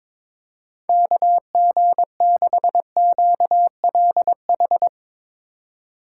Morse code, heard as a keyed tone, about 23 wpm, 700 Hz. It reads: KG6QLH
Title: Speed Racing - USA Call Signs 22wpm